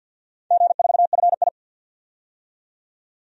Morse code, heard as a keyed tone, about 48 wpm, 700 Hz. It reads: G4FI
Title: DXCC-100 Call Signs - 4 Characters - 50wpm